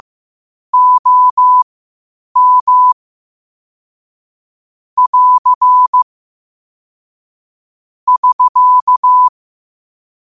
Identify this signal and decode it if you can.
Morse code, heard as a keyed tone, about 15 wpm, spaced as for 8 wpm, 1000 Hz. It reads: OM <AR> <SK>
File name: qotdfiles / 8wpm